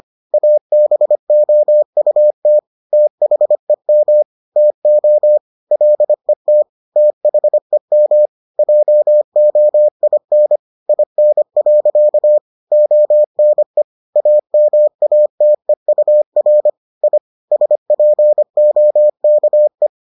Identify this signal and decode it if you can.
Morse code, heard as a keyed tone, about 25 wpm, 600 Hz. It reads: ABOUT THEM TO LET THEM JOIN IN. ONE AMATEUR I SPOKE